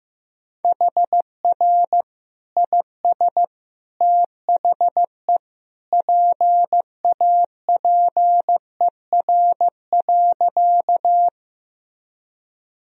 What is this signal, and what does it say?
Morse code, heard as a keyed tone, about 16 wpm, 700 Hz. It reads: HR IS THE PAPER.
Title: Sentences from Top 300 Words and Abbreviations 15wpm